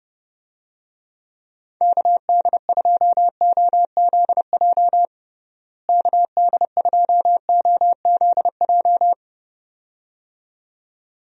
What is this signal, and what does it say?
Morse code, heard as a keyed tone, about 31 wpm, 700 Hz. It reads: KD2OZJ KD2OZJ
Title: USA Call Signs - 6 Characters - Repeated 2x - 30wpm